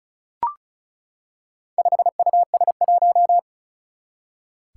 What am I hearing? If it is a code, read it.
Morse code, heard as a keyed tone, about 36 wpm, 700 Hz. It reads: 5US1
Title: Four Challenging Letter-Number Combinations 35wpm